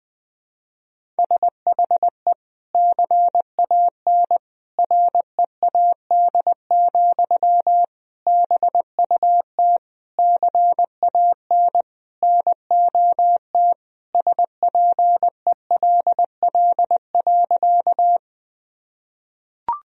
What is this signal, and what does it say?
Morse code, heard as a keyed tone, about 20 wpm, 700 Hz. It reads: SHE CAN READ, BUT CAN NOT SPELL.
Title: Sentences from Top 1000 Words - Review Part 1 - 20wpm